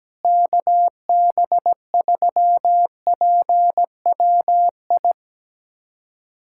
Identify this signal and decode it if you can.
Morse code, heard as a keyed tone, about 17 wpm, 700 Hz. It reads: KB3PWI
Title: USA Callsign DE Callsign 17wpm